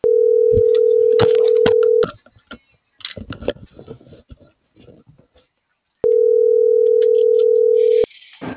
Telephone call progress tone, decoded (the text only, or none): ringback tone